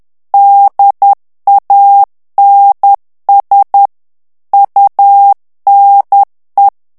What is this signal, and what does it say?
Morse code, heard as a keyed tone, about 10 wpm, 800 Hz. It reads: DANS UNE
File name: FRANCAIS TEXTE CLAIR 2 VITESSE 10 MOTS MINUTES